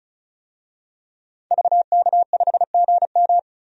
Morse code, heard as a keyed tone, about 35 wpm, 700 Hz. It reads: VK5GM
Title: DXCC-100 Call Signs - 5 Characters - 35wpm